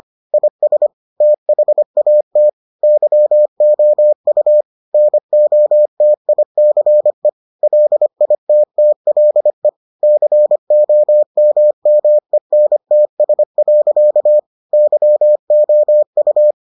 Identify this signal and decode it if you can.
Morse code, heard as a keyed tone, about 25 wpm, 600 Hz. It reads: IS THAT YOU NOTICE LITTLE COMMENTS. YOU